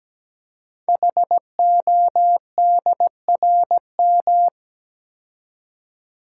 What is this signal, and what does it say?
Morse code, heard as a keyed tone, about 17 wpm, 700 Hz. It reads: HODRM
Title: Code Groups - 5 Letters - 17wpm